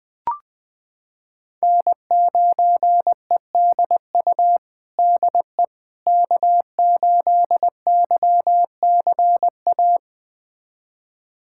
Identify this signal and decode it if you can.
Morse code, heard as a keyed tone, about 20 wpm, 700 Hz. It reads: N9EDU DE K8YCA
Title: USA Callsign DE Callsign - 5 Characters - 20wpm